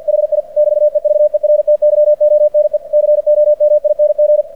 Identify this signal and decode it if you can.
Morse code, heard as a keyed tone, about 46 wpm, 600 Hz. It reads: HI <BT> EVERTYONE COMING
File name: archived / CRSnet